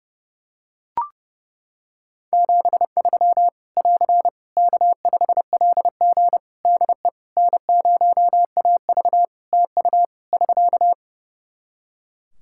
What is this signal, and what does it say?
Morse code, heard as a keyed tone, about 30 wpm, 700 Hz. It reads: 73 <AR> K5LG DE N0AV TU <SK>
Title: K1USN SST QSO Practice 30wpm